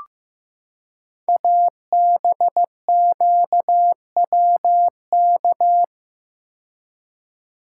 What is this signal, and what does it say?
Morse code, heard as a keyed tone, about 15 wpm, 700 Hz. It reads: ABQWK
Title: Code Groups - 5 Letters - 15wpm